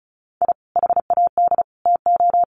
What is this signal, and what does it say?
Morse code, heard as a keyed tone, about 34 wpm, 700 Hz. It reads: I HAD TO
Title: Sets of 3 Words 35wpm